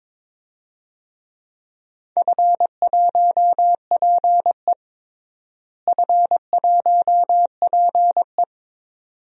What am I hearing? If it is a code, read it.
Morse code, heard as a keyed tone, about 22 wpm, 700 Hz. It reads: F1PE F1PE
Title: DXCC-100 Call Signs - 4 Characters - Repeated 2x - 22wpm